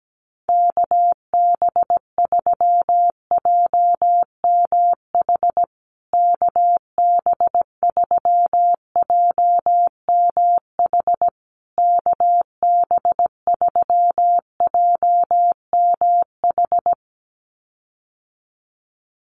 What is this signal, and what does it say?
Morse code, heard as a keyed tone, about 17 wpm, 700 Hz. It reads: KB3JMH KB3JMH KB3JMH
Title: USA Call Signs Repeated 3x 17wpm